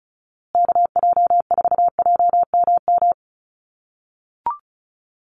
Morse code, heard as a keyed tone, about 34 wpm, 700 Hz. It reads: KJ4JMM
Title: USA Call Signs Repeated 3x 35wpm